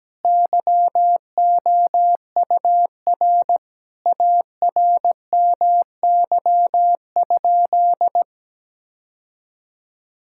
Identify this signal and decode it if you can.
Morse code, heard as a keyed tone, about 17 wpm, 700 Hz. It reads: YOUR ARMY?
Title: Sentences from Top 1000 Words - Review Part 5 - 17wpm